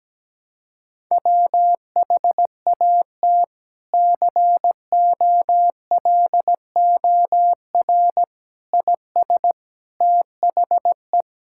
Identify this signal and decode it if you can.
Morse code, heard as a keyed tone, about 17 wpm, 700 Hz. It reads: WHAT COLOR IS THE
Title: Sentences from Top 1000 Words - Review Part 6 - 17wpm